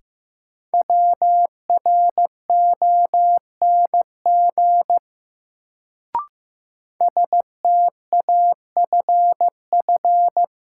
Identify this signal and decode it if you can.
Morse code, heard as a keyed tone, about 15 wpm, 700 Hz. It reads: WRONG STAFF
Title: Sets of 5 Words Encore - New Word Drill - 15wpm